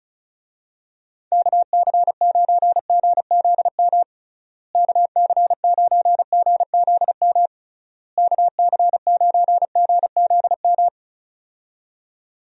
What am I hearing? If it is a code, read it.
Morse code, heard as a keyed tone, about 36 wpm, 700 Hz. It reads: KC9GZM KC9GZM KC9GZM
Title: USA Call Signs - 6 Characters - Repeated 3x - 35wpm